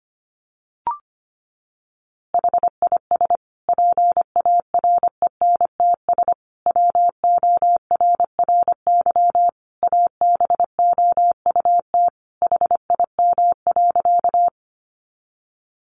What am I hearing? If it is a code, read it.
Morse code, heard as a keyed tone, about 25 wpm, 700 Hz. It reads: HIS PARENTS WORRY ABOUT HIM.
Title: Easy Phrases 25wpm